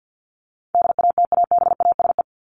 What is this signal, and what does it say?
Morse code, heard as a keyed tone, about 48 wpm, 700 Hz. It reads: DATABASE
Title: Top 1751-2000 Words 50wpm